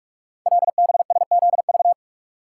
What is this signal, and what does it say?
Morse code, heard as a keyed tone, about 44 wpm, 700 Hz. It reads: LBSZV
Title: Code Groups - 5 Letters - 45wpm